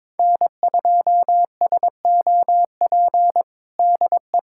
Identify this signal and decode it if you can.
Morse code, heard as a keyed tone, about 22 wpm, 700 Hz. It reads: N2SOP DE
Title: USA Callsign DE Callsign - 5 Characters - 22wpm